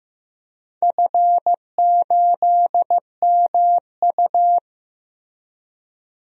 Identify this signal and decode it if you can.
Morse code, heard as a keyed tone, about 15 wpm, 700 Hz. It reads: F8MU
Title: DXCC-100 Call Signs - 4 Characters - 15wpm